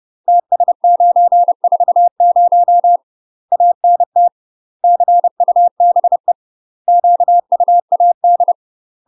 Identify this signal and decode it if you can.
Morse code, heard as a keyed tone, about 30 wpm, 700 Hz. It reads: TS940 ANT CUBE QUAD
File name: UK MASTER QSOs TRAINING / 30WPM / UK MASTER QSOs 0015